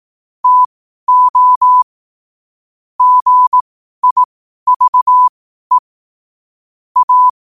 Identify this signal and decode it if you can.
Morse code, heard as a keyed tone, about 18 wpm, spaced as for 12 wpm, 1000 Hz. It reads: TO GIVE A
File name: qotdfiles / 12wpm